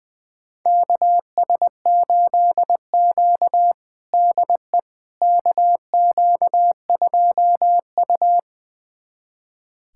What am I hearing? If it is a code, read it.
Morse code, heard as a keyed tone, about 20 wpm, 700 Hz. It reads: KS8Q DE KQ2U
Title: USA Callsign DE Callsign - 4 Characters - 20wpm